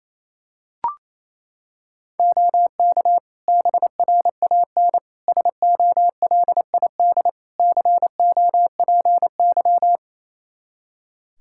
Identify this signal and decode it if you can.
Morse code, heard as a keyed tone, about 28 wpm, 700 Hz. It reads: OK BRAN SOLID COPY
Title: Practice QSO - Exchange 3 - More Info - 28wpm